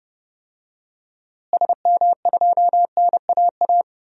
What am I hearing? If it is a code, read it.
Morse code, heard as a keyed tone, about 30 wpm, 700 Hz. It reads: SM2NAA
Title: DXCC-100 Call Signs - 6 Characters - Repeated 3x - 30wpm